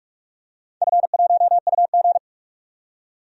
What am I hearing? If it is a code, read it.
Morse code, heard as a keyed tone, about 44 wpm, 700 Hz. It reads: F1UG
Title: DXCC-100 Call Signs - 4 Characters - 45wpm